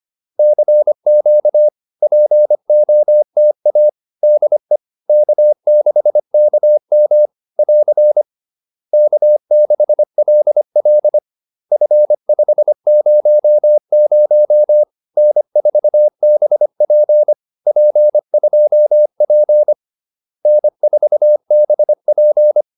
Morse code, heard as a keyed tone, about 25 wpm, 600 Hz. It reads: CQ POTA DE K6KM <AR> K6LL F500 N4BP P2P N4BP
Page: advanced-practice-files